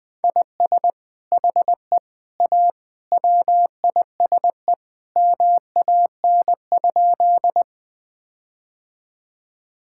Morse code, heard as a keyed tone, about 20 wpm, 700 Hz. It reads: IS HE A WISE MAN?
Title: Sentences from Top 1600 Words 20wpm